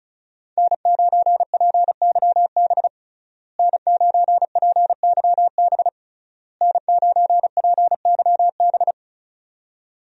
Morse code, heard as a keyed tone, about 34 wpm, 700 Hz. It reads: N9PYB N9PYB N9PYB
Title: USA Call Signs - 5 Characters - Repeated 3x - 35wpm